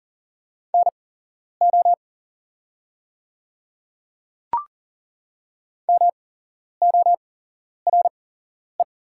Morse code, heard as a keyed tone, about 40 wpm, spaced as for 11 wpm, 700 Hz. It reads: NO MORE
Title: 8x Character Spacing - Top 500 Words - 40wpm